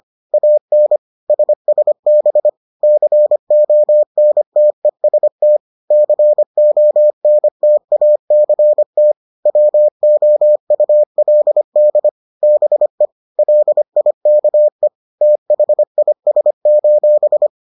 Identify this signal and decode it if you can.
Morse code, heard as a keyed tone, about 25 wpm, 600 Hz. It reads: AN SSB CONTEST CONTACT WOULD BE LIKE THIS: